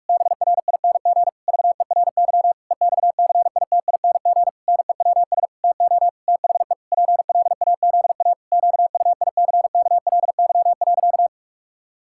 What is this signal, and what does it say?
Morse code, heard as a keyed tone, about 46 wpm, 700 Hz. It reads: BRING VERY EXCITING NEWS TO THE PLAZA QUICKLY.
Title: Pangram Challenge v2 - 45wpm